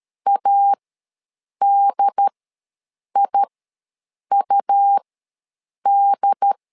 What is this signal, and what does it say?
Morse code, heard as a keyed tone, about 13 wpm, spaced as for 7 wpm, 800 Hz. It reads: ADIUD
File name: LEZIONI / 2 UHDN / 2 UHD ws 12wpm 1722